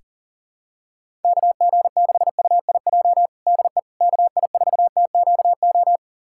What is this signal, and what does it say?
Morse code, heard as a keyed tone, about 40 wpm, 700 Hz. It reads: KG6UIJ DE KI4TQO
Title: USA Callsign DE Callsign - 6 Characters - 40wpm